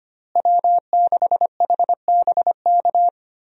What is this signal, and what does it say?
Morse code, heard as a keyed tone, about 25 wpm, 700 Hz. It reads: W6HBK
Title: USA Call Signs - 5 Characters - Repeated 3x - 25wpm